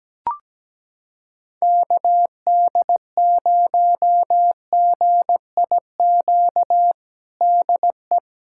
Morse code, heard as a keyed tone, about 17 wpm, 700 Hz. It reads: KD0GIQ DE
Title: USA Callsign DE Callsign - 6 Characters - 17wpm